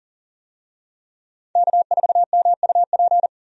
Morse code, heard as a keyed tone, about 37 wpm, 700 Hz. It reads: K4MUP
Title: USA Call Signs - 5 Characters - 40wpm